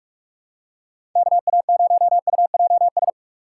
Morse code, heard as a keyed tone, about 47 wpm, 700 Hz. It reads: KA0UJS
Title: USA Call Signs - 2x Character Spacing - 45wpm